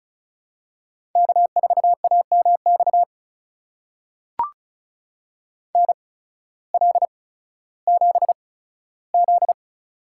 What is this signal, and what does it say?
Morse code, heard as a keyed tone, about 36 wpm, spaced as for 16 wpm, 700 Hz. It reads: K4AMX NL7Z
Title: USA Call Signs - 8x Character Spacing - 35wpm